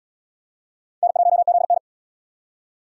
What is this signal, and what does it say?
Morse code, heard as a keyed tone, about 68 wpm, 700 Hz. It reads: N1ZN